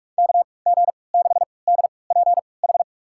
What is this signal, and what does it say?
Morse code, heard as a keyed tone, about 48 wpm, spaced as for 26 wpm, 700 Hz. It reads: KG6DPH
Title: USA Call Signs - 6 Characters - 3x Character Spacing - 45wpm